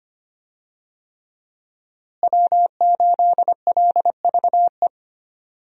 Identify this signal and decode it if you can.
Morse code, heard as a keyed tone, about 25 wpm, 700 Hz. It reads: W8LVE
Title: USA Call Signs - 5 Characters - Repeated 3x - 25wpm